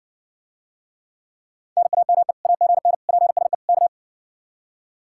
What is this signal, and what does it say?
Morse code, heard as a keyed tone, about 60 wpm, 700 Hz. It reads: NAME AGN PSE K